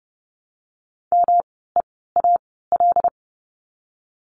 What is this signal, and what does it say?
Morse code, heard as a keyed tone, about 30 wpm, spaced as for 17 wpm, 700 Hz. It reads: MEAL
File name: course / 30:10wpm / 108 Lesson 18f - Letter m - Words 30wpm